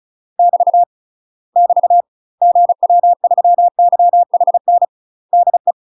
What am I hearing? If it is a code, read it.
Morse code, heard as a keyed tone, about 36 wpm, 700 Hz. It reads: <BT> <BT> GW3YHN DE